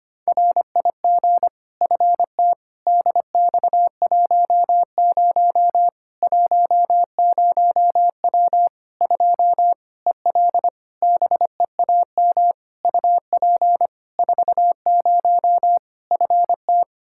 Morse code, heard as a keyed tone, about 25 wpm, 700 Hz. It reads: RIG FT DX10 10W 2 EL BEAM UP 40 FT
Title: Practice QSO - Exchange 2 - Info - 25wpm